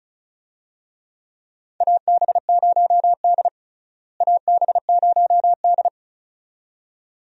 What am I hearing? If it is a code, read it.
Morse code, heard as a keyed tone, about 36 wpm, 700 Hz. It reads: AB0D AB0D
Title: USA Call Signs - 4 Characters - Repeated 2x - 35wpm